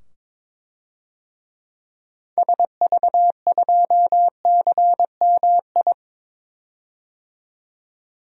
Morse code, heard as a keyed tone, about 21 wpm, 700 Hz. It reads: SV2CMI
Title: DXCC-100 Call Signs - 6 Characters - 22wpm